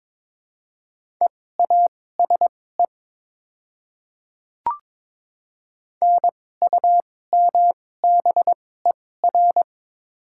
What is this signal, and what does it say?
Morse code, heard as a keyed tone, about 22 wpm, spaced as for 16 wpm, 700 Hz. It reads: EASE NUMBER
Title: 2x Character Spacing - Top 300 Words - 22wpm